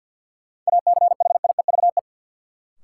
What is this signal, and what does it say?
Morse code, heard as a keyed tone, about 52 wpm, 700 Hz. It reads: ACHIEVE
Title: Sentences from Top 1000 Words - Review Part 5 - New Word Drill - 50wpm